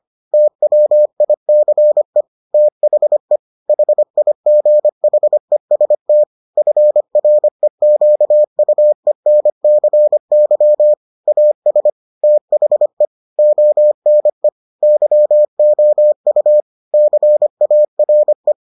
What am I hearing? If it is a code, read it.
Morse code, heard as a keyed tone, about 25 wpm, 600 Hz. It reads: TWICE THE HIGHEST FREQUENCY AS THE ONE YOU CARE